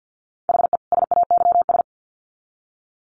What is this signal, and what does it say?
Morse code, heard as a keyed tone, about 50 wpm, 700 Hz. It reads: HE SAYS
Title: Sets of 2 Words 50wpm